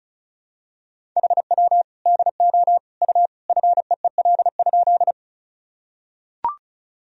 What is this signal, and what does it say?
Morse code, heard as a keyed tone, about 36 wpm, 700 Hz. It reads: HW DO U FEEL?
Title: Sentences from Top 400 Words and Abbreviations 35wpm